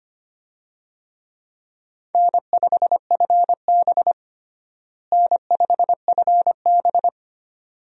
Morse code, heard as a keyed tone, about 25 wpm, 700 Hz. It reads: N5FB N5FB
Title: USA Call Signs - 4 Characters - Repeated 2x - 25wpm